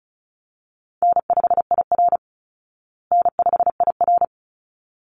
Morse code, heard as a keyed tone, about 34 wpm, 700 Hz. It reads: N5IR N5IR
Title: Easy Call Signs Repeated 2x - 35wpm